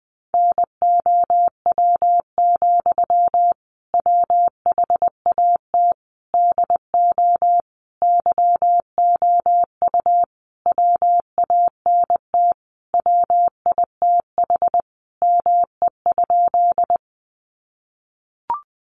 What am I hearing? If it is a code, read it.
Morse code, heard as a keyed tone, about 20 wpm, 700 Hz. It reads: NOW, WHAT DO YOU WANT WITH ME?
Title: Sentences from Top 200 Words 20wpm